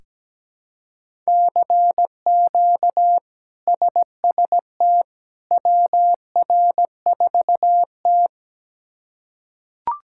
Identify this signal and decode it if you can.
Morse code, heard as a keyed tone, about 17 wpm, 700 Hz. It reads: CQ SST WR4T
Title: K1USN SST QSO Practice - 17wpm